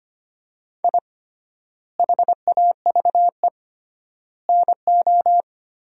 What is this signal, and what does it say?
Morse code, heard as a keyed tone, about 25 wpm, 700 Hz. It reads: I HAVE NO
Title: 3x Word-Spacing - Sentences from Top 400 Words - 25wpm